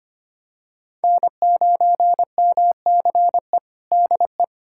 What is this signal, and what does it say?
Morse code, heard as a keyed tone, about 25 wpm, 700 Hz. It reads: N9MCE DE
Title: USA Callsign DE Callsign - 5 Characters - 25wpm